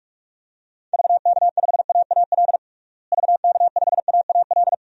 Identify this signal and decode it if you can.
Morse code, heard as a keyed tone, about 46 wpm, 700 Hz. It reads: VK5AAL VK5AAL
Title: DXCC-100 Call Signs - 6 Characters - Repeated 2x - 45wpm